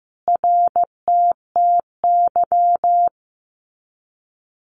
Morse code, heard as a keyed tone, about 15 wpm, 700 Hz. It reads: RTTY
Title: Abbreviations Repeated 3x - 15wpm